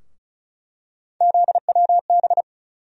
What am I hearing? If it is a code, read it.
Morse code, heard as a keyed tone, about 34 wpm, 700 Hz. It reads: ZWB
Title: USA Call Sign Postfixes - 3 Letters 35wpm